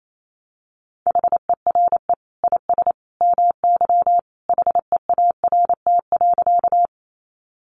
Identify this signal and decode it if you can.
Morse code, heard as a keyed tone, about 28 wpm, 700 Hz. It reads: HERE IS MY HEART.
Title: Sentences from Top 1200 Words 28wpm